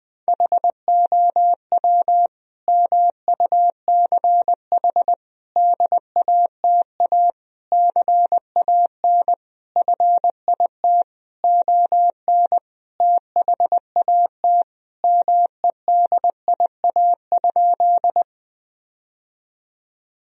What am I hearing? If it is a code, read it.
Morse code, heard as a keyed tone, about 19 wpm, 700 Hz. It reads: HOW MUCH DATA CAN FIT ON THAT MEDIA?
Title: Sentences from Top 1000 Words - Review Part 4 - 20wpm